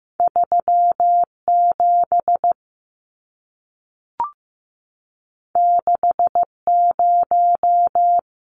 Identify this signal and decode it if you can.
Morse code, heard as a keyed tone, about 15 wpm, 700 Hz. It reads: 37 60
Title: Numbers - 2-Digit 15wpm